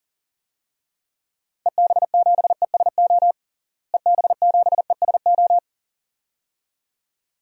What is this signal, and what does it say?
Morse code, heard as a keyed tone, about 40 wpm, 700 Hz. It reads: EB7ESO EB7ESO
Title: DXCC-100 Call Signs - 6 Characters - Repeated 2x - 40wpm